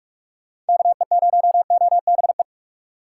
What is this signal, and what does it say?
Morse code, heard as a keyed tone, about 67 wpm, spaced as for 55 wpm, 700 Hz. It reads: KE0OBE
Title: Speed Racing - USA Call Signs 45wpm